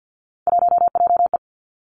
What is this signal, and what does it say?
Morse code, heard as a keyed tone, about 50 wpm, 700 Hz. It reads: 1JE
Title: Three Challenging Letter-Number Combinations 50wpm